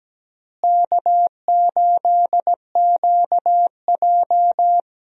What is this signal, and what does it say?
Morse code, heard as a keyed tone, about 17 wpm, 700 Hz. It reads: K8QJ
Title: USA Call Signs - 4 Characters - Repeated 3x - 17wpm